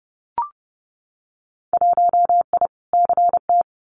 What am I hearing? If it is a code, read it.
Morse code, heard as a keyed tone, about 31 wpm, 700 Hz. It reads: 1I CT
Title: Winter Field Day Exchanges 30wpm